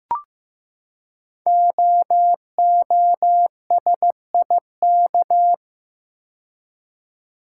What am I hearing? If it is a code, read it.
Morse code, heard as a keyed tone, about 15 wpm, 700 Hz. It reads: OOSIK
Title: Code Groups - 5 Letters - 15wpm